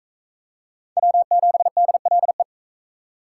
Morse code, heard as a keyed tone, about 43 wpm, 700 Hz. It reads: W7DLE